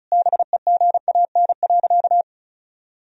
Morse code, heard as a keyed tone, about 34 wpm, 700 Hz. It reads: BEGAN.
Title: Sentences from Top 1000 Words - Review Part 1 - 35wpm